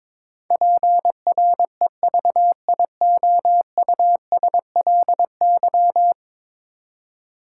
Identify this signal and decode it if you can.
Morse code, heard as a keyed tone, about 23 wpm, 700 Hz. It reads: PREVIOUSLY
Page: morse-code-ninja-practice-22wpm